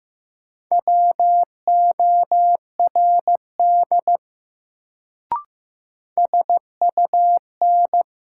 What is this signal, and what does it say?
Morse code, heard as a keyed tone, about 16 wpm, 700 Hz. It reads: WORD SUN
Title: Sentences from Top 300 Words - New Word Drill - 15wpm